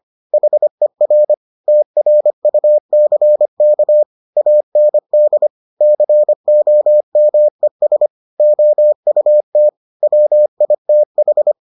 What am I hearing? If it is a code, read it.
Morse code, heard as a keyed tone, about 25 wpm, 600 Hz. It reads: HER TRUCK AND COMES OUT WITH